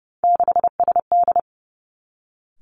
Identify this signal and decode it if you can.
Morse code, heard as a keyed tone, about 30 wpm, 700 Hz. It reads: 6SD
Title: Three Challenging Letter-Number Combinations 30wpm